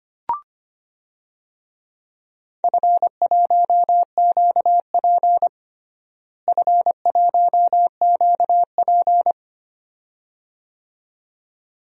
Morse code, heard as a keyed tone, about 25 wpm, 700 Hz. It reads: F1QP F1QP
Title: DXCC-100 Call Signs - 4 Characters - Repeated 2x - 25wpm